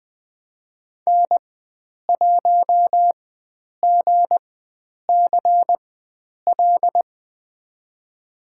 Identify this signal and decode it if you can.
Morse code, heard as a keyed tone, about 20 wpm, spaced as for 9 wpm, 700 Hz. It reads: N1GCL
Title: USA Call Signs - 5 Characters - 4x Character Spacing - 20wpm